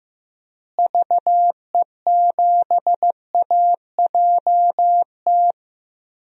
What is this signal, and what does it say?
Morse code, heard as a keyed tone, about 15 wpm, 700 Hz. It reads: VE7AJT
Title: DXCC-100 Call Signs - 6 Characters - Repeated 2x - 15wpm